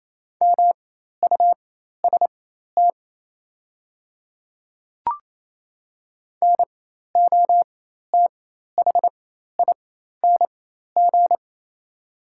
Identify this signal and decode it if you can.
Morse code, heard as a keyed tone, about 28 wpm, spaced as for 13 wpm, 700 Hz. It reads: MUST NOTHING
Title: 4x Character Spacing - Top 500 Words - 28wpm